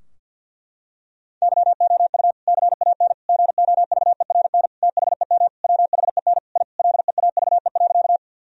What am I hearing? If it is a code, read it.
Morse code, heard as a keyed tone, about 49 wpm, 700 Hz. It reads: YOU CAN GOVERN THEM WHEN I LEAVE.
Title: Sentences from Top 1000 Words - Review Part 2 - 50wpm